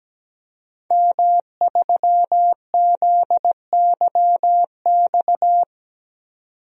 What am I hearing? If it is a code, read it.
Morse code, heard as a keyed tone, about 17 wpm, 700 Hz. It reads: M3ZYX
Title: DXCC-100 Call Signs - 5 Characters - 17wpm